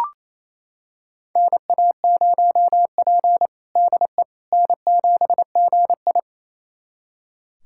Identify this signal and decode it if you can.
Morse code, heard as a keyed tone, about 28 wpm, 700 Hz. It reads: NA0P DE N7GI
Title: USA Callsign DE Callsign - 4 Characters - 28wpm